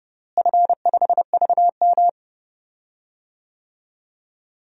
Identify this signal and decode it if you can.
Morse code, heard as a keyed tone, about 30 wpm, 700 Hz. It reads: F5VM
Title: DXCC-100 Call Signs - 4 Characters - Repeated 3x - 30wpm